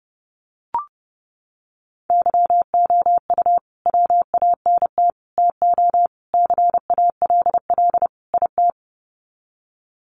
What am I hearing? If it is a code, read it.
Morse code, heard as a keyed tone, about 30 wpm, 700 Hz. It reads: YOU WANT TO CALL IT
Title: Sets of 5 Words Encore2 30wpm